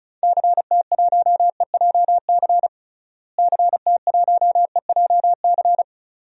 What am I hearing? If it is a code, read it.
Morse code, heard as a keyed tone, about 34 wpm, 700 Hz. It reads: CT1EJC CT1EJC
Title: DXCC-100 Call Signs - 6 Characters - Repeated 2x - 35wpm